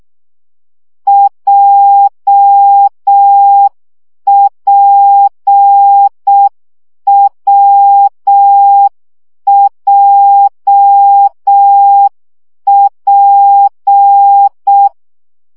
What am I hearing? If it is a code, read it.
Morse code, heard as a keyed tone, about 6 wpm, 800 Hz. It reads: JPWJP
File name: CW gyakorlas w p j 5 baud